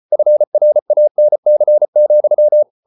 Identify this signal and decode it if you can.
Morse code, heard as a keyed tone, about 34 wpm, 600 Hz. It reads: FRANC,
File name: Ferie QRQ 34wpm